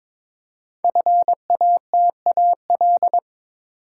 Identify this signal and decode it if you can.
Morse code, heard as a keyed tone, about 22 wpm, 700 Hz. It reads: FATAL
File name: 5 Letter Words - First 3 Letters Given - 22wpm